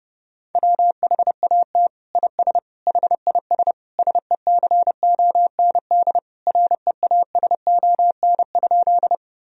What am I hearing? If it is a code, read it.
Morse code, heard as a keyed tone, about 30 wpm, 700 Hz. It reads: WHAT IS HIS SECOND REASON?
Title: Sentences from Top 1000 Words - Review Part 1 - 30wpm